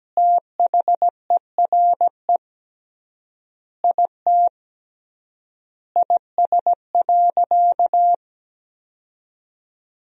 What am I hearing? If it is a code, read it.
Morse code, heard as a keyed tone, about 17 wpm, 700 Hz. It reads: THERE IT IS.
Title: 3x Word-Spacing - Sentences from Top 500 Words - 17wpm